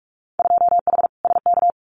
Morse code, heard as a keyed tone, about 44 wpm, 700 Hz. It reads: 2H SK
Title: Winter Field Day Exchanges Encore 45wpm